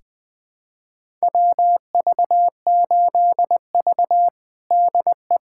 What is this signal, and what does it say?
Morse code, heard as a keyed tone, about 20 wpm, 700 Hz. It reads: WV8V DE
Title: USA Callsign DE Callsign - 4 Characters - 20wpm